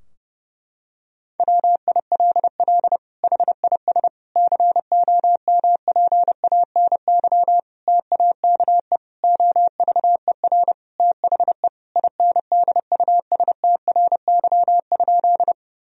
Morse code, heard as a keyed tone, about 30 wpm, 700 Hz. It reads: WILL HIS COMPANY TAKE OVER THE INDUSTRY?
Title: Sentences from Top 1000 Words - Review Part 4 - 30wpm